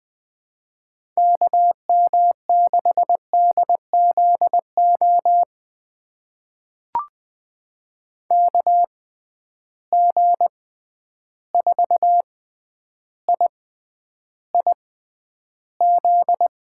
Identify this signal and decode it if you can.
Morse code, heard as a keyed tone, about 20 wpm, spaced as for 10 wpm, 700 Hz. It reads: KM6DZO KG4IIZ
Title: USA Call Signs - 6 Characters - 6x Character Spacing - 20wpm